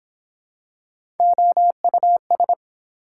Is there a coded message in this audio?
Yes: Morse code from 1.2 to 2.5 s, heard as a keyed tone, about 24 wpm, 700 Hz. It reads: OUS